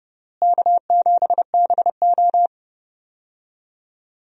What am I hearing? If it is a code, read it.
Morse code, heard as a keyed tone, about 30 wpm, 700 Hz. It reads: K7BO
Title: USA Call Signs - 4 Characters - Repeated 3x - 30wpm